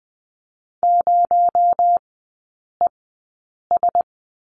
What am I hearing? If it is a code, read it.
Morse code, heard as a keyed tone, about 20 wpm, spaced as for 8 wpm, 700 Hz. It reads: 0ES
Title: 218 Lesson 35e - Number 0 - Three Characters 20wpm